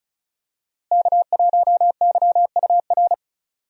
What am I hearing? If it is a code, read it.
Morse code, heard as a keyed tone, about 36 wpm, 700 Hz. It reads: K1YUR
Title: USA Call Signs - 5 Characters - Repeated 3x - 35wpm